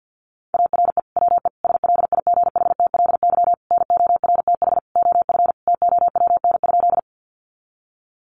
Morse code, heard as a keyed tone, about 48 wpm, 700 Hz. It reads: ARE WE SLIGHTLY NORTH OF TOWN?
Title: Sentences from Top 1300 Words 50wpm